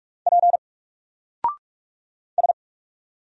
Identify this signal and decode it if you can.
Morse code, heard as a keyed tone, about 44 wpm, 700 Hz. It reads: P S
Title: Single Character (Letter:Number:Punctuation:Prosign) Alt 45wpm